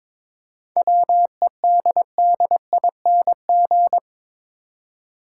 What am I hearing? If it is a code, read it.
Morse code, heard as a keyed tone, about 22 wpm, 700 Hz. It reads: WEDDING
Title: Seven Letter Words - 22wpm